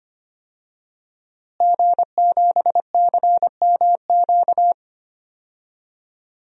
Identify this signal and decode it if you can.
Morse code, heard as a keyed tone, about 25 wpm, 700 Hz. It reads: G7CMQ
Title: DXCC-100 Call Signs - 5 Characters - Repeated 2x - 25wpm